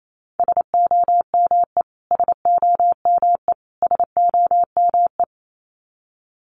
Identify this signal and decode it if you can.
Morse code, heard as a keyed tone, about 28 wpm, 700 Hz. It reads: SOME SOME SOME
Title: Top 500 Words Repeated 28wpm